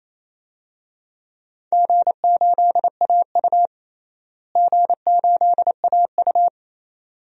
Morse code, heard as a keyed tone, about 27 wpm, 700 Hz. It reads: G8AU G8AU
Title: DXCC-100 Call Signs - 4 Characters - Repeated 2x - 28wpm